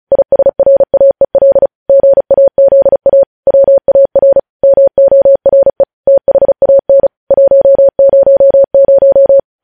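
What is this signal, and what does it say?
Morse code, heard as a keyed tone, about 36 wpm, 550 Hz. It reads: ISRAEL GAZA WAR MORE THAN 100
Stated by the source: News headlines sent in Morse at speed 35.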